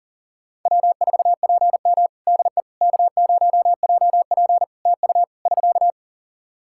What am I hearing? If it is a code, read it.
Morse code, heard as a keyed tone, about 40 wpm, 700 Hz. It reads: W4PM DE K0JP TU <SK>
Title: Practice QSO - Exchange 4 - Ending - 40wpm